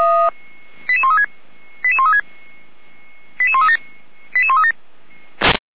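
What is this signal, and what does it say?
Five-tone selective call (ZVEI), heard as quick runs of five tones, tones of about 70 ms. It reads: 88136 88136 88137 88137
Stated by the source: ZVEI five-tone selective call received on SDR, recorded by feuerwehr-orlamuende.org fire brigade.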